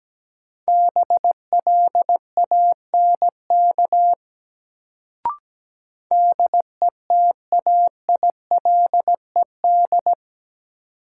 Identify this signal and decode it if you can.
Morse code, heard as a keyed tone, about 17 wpm, 700 Hz. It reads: BLANK DETAILED
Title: Sentences from Top 1500 Words - New Word Drill Alt - 17wpm